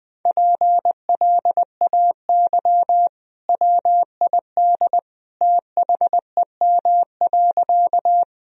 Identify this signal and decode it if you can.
Morse code, heard as a keyed tone, about 20 wpm, 700 Hz. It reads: PLAY WID THEM.
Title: Sentences from Top 200 Words and Abbreviations 20wpm